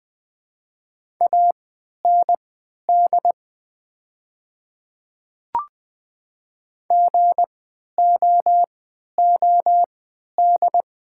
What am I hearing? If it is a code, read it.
Morse code, heard as a keyed tone, about 21 wpm, spaced as for 12 wpm, 690 Hz. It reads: AND GOOD
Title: 3x Character Spacing - Top 300 Words - 20wpm